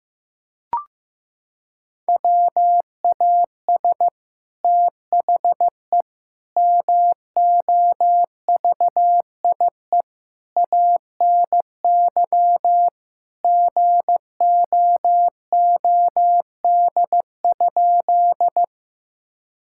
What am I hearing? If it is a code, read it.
Morse code, heard as a keyed tone, about 16 wpm, 700 Hz. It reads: WAS THE MOVIE ANY GOOD?
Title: Sentences from Top 1000 Words - Review Part 5 - 15wpm